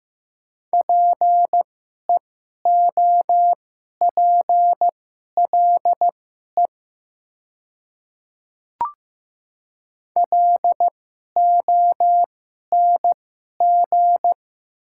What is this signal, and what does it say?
Morse code, heard as a keyed tone, about 15 wpm, spaced as for 11 wpm, 700 Hz. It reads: PEOPLE LONG
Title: 2x Character Spacing - Top 100 Words - 15wpm